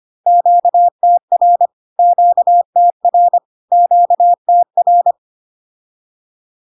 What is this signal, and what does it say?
Morse code, heard as a keyed tone, about 25 wpm, 700 Hz. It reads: QTR QTR QTR
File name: Q-codes most common repeated 3 times 25 25wpm